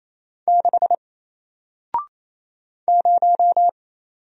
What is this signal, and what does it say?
Morse code, heard as a keyed tone, about 28 wpm, 700 Hz. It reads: 6 0
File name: Single-Letter-Number-28wpm